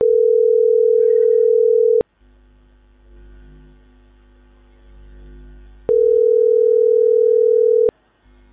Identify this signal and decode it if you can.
Ringback tone (North America), 440 + 480 Hz.